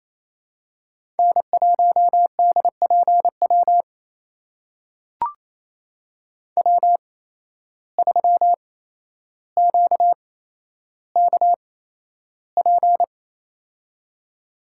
Morse code, heard as a keyed tone, about 28 wpm, spaced as for 12 wpm, 700 Hz. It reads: N1DPW W3QKP
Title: USA Call Signs - 5 Characters - 8x Character Spacing - 28wpm